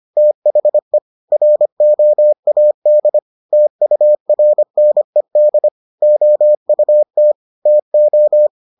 Morse code, heard as a keyed tone, about 25 wpm, 600 Hz. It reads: THE ROAD TURNED OUT TO